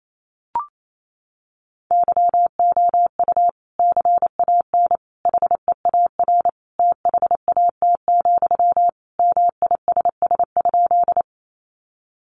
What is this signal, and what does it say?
Morse code, heard as a keyed tone, about 28 wpm, 700 Hz. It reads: YOU CAN HEAR THAT, MISS?
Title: Sentences from Top 500 Words 28wpm